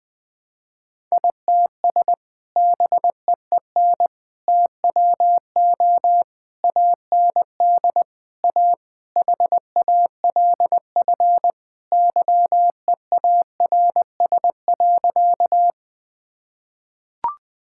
Morse code, heard as a keyed tone, about 21 wpm, 700 Hz. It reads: ITS BEEN TWO AND A HALF YEARS.
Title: Sentences with 5-Word Phrases - Spoken Beginning - 20wpm